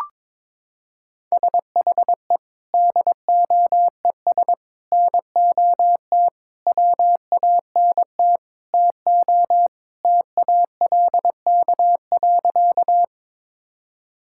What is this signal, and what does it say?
Morse code, heard as a keyed tone, about 22 wpm, 700 Hz. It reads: SHE DOES NOT WANT TO TALK.
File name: practice / 22wpm / Sentences with 4-Word Phrases - 22wpm